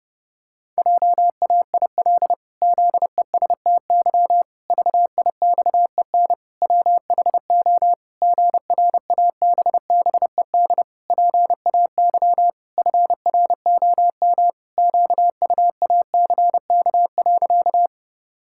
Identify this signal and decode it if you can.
Morse code, heard as a keyed tone, about 30 wpm, 700 Hz. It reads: JAIL ZESTY VIXEN WHO GRABBED PAY FROM QUACK.
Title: Pangram Challenge v2 - 30wpm